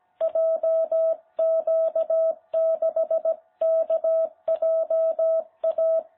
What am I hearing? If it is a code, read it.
Morse code, heard as a keyed tone, about 17 wpm, 660 Hz. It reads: JQ6KJA
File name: jh8cbh / 2021cw